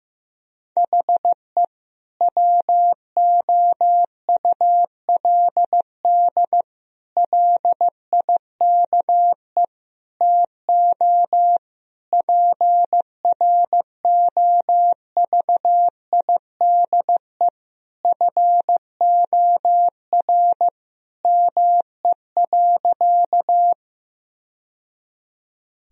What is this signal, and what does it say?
Morse code, heard as a keyed tone, about 15 wpm, 700 Hz. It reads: HE WOULD LIKE TO PROVIDE FOR ME.
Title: Sentences from Top 1000 Words - Review Part 3 - 15wpm